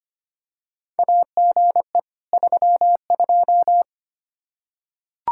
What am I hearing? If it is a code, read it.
Morse code, heard as a keyed tone, about 25 wpm, 700 Hz. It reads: AGE 32
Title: Practice QSO - Exchange 3 - More Info - 25wpm